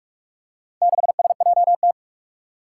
Morse code, heard as a keyed tone, about 43 wpm, 700 Hz. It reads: 6SJT
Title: Four Challenging Letter-Number Combinations 45wpm